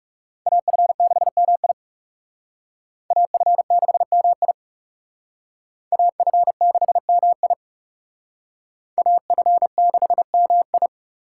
Morse code, heard as a keyed tone, about 39 wpm, 700 Hz. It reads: AF6MI AF6MI AF6MI AF6MI